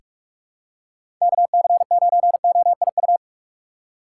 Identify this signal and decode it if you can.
Morse code, heard as a keyed tone, about 47 wpm, 700 Hz. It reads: KC9OIU
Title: USA Call Signs - 6x Character Spacing - 45wpm